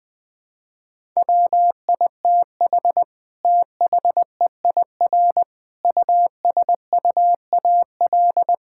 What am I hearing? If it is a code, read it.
Morse code, heard as a keyed tone, about 19 wpm, 700 Hz. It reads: WITH THEIR USUAL
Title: Sentences from Top 1000 Words - Review Part 6 - 20wpm